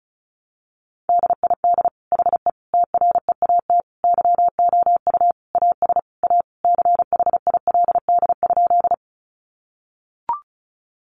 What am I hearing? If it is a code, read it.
Morse code, heard as a keyed tone, about 34 wpm, 700 Hz. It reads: DID HE TREAT YOU AS A CHILD?
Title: Sentences from Top 1300 Words 35wpm